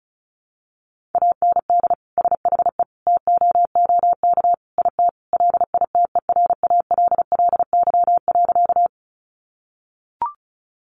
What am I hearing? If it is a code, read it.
Morse code, heard as a keyed tone, about 34 wpm, 700 Hz. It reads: AND SHE TOOK IT LITERALLY.
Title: Sentences from Top 1400 Words 35wpm